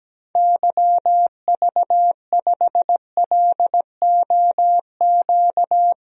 Morse code, heard as a keyed tone, about 17 wpm, 700 Hz. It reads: YV5LOQ
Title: DXCC-100 Call Signs - 6 Characters - Repeated 2x - 17wpm